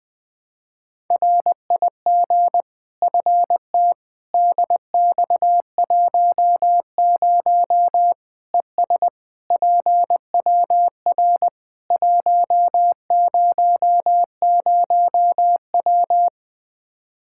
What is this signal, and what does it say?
Morse code, heard as a keyed tone, about 20 wpm, 700 Hz. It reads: RIG FT DX10 ES PWR 100W
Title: Practice QSO - Exchange 2 - Info - 20wpm